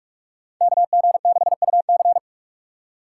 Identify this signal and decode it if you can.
Morse code, heard as a keyed tone, about 59 wpm, spaced as for 52 wpm, 700 Hz. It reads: KG6UC
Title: Speed Racing - USA 2x2 Call Signs 45wpm